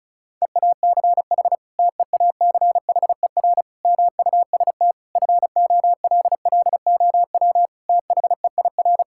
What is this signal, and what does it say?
Morse code, heard as a keyed tone, about 36 wpm, 700 Hz. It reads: EACH TEACHER MUST FOLLOW THEIR
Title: Sentences from Top 1000 Words - Review Part 6 - 35wpm